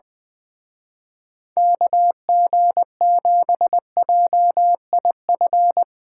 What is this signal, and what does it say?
Morse code, heard as a keyed tone, about 24 wpm, 700 Hz. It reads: KG7JIF
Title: Speed Racing - USA Call Signs 20wpm